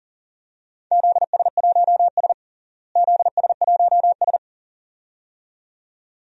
Morse code, heard as a keyed tone, about 40 wpm, 700 Hz. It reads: ZS1S ZS1S
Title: DXCC-100 Call Signs - 4 Characters - Repeated 2x - 40wpm